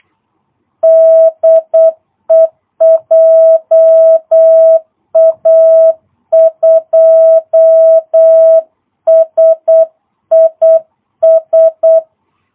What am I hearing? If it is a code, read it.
Morse code, heard as a keyed tone, about 8 wpm, 650 Hz. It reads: DE JA2SIS
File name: jh8cbh / 2022cw